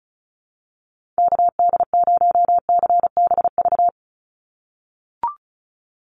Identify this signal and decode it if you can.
Morse code, heard as a keyed tone, about 36 wpm, 700 Hz. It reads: KD0CBV
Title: USA Call Signs 35wpm